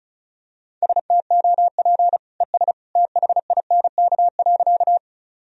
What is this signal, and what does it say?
Morse code, heard as a keyed tone, about 36 wpm, 700 Hz. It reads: STOP ES THINK.
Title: Sentences from Top 300 Words and Abbreviations 35wpm